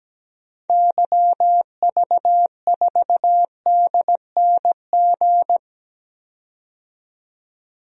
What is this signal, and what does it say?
Morse code, heard as a keyed tone, about 17 wpm, 700 Hz. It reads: YV4DNG
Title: DXCC-100 Call Signs - 6 Characters - 17wpm